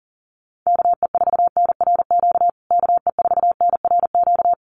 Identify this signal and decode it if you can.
Morse code, heard as a keyed tone, about 40 wpm, 700 Hz. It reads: KE4NRQ KE4NRQ
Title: USA Call Signs Repeated 2x 40wpm